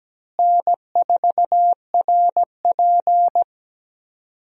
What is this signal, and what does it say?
Morse code, heard as a keyed tone, about 17 wpm, 700 Hz. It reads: N4RP
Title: USA Call Signs - 4 Characters - Repeated 3x - 17wpm